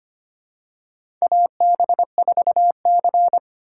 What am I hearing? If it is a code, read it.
Morse code, heard as a keyed tone, about 25 wpm, 700 Hz. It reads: AB4C
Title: USA Call Signs - 4x Character Spacing - 25wpm